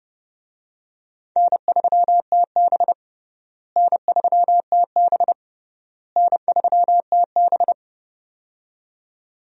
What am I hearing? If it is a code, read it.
Morse code, heard as a keyed tone, about 30 wpm, 700 Hz. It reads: N3TB N3TB N3TB
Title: USA Call Signs - 4 Characters - Repeated 3x - 30wpm